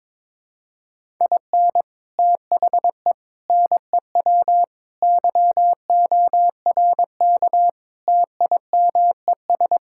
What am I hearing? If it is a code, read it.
Morse code, heard as a keyed tone, about 23 wpm, 700 Hz. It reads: IN THE NEW YORK TIMES
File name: practice / 22wpm / Build A Phrase - 5-Word Phrases - 22wpm